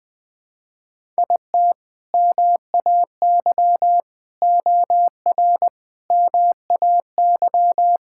Morse code, heard as a keyed tone, about 21 wpm, 700 Hz. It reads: IT MAY OR MAY
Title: Sentences with 5-Word Phrases - Spoken Beginning - 20wpm